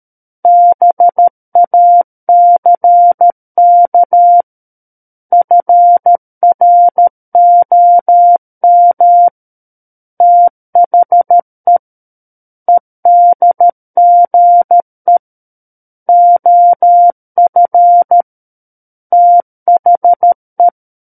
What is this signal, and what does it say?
Morse code, heard as a keyed tone, about 13 wpm, 700 Hz. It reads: BACK FROM THE EDGE OF THE
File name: Book One / 04-Chapter 4 - 13 WPM